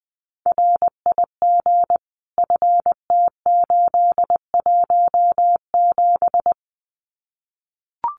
Morse code, heard as a keyed tone, about 20 wpm, 700 Hz. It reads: RIG FT817
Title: Common QSO Phrases Encore 20wpm